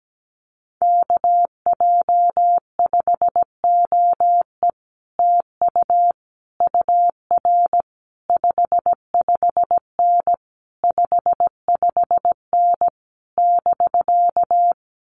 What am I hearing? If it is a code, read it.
Morse code, heard as a keyed tone, about 17 wpm, 700 Hz. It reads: KJ5OE TU UR 55N 55N <BK>
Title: Common QSO Phrases Encore 17wpm